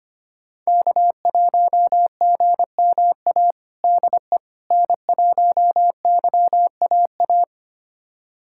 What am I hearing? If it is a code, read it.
Morse code, heard as a keyed tone, about 25 wpm, 700 Hz. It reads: K1GMA DE N1YAA
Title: USA Callsign DE Callsign - 5 Characters - 25wpm